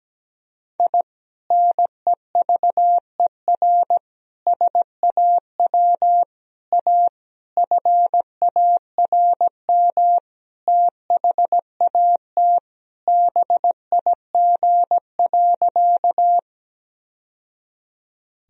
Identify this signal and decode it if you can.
Morse code, heard as a keyed tone, about 17 wpm, 700 Hz. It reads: I NEVER SAW A FARM THAT BIG.
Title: Sentences from Top 1000 Words - Review Part 2 - 17wpm